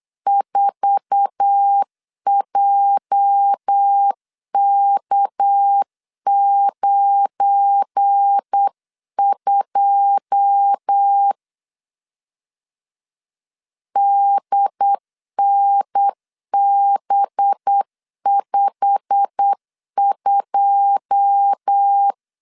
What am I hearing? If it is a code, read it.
Morse code, heard as a keyed tone, about 8 wpm, 800 Hz. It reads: 4JK92 DNB52
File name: CW / DET / 1 9 1500C S C 8wpm 5653